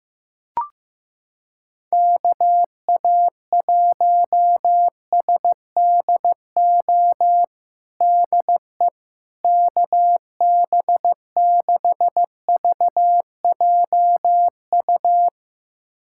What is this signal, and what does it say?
Morse code, heard as a keyed tone, about 16 wpm, 700 Hz. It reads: KA1SDO DE KB6VJU
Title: USA Callsign DE Callsign - 6 Characters - 15wpm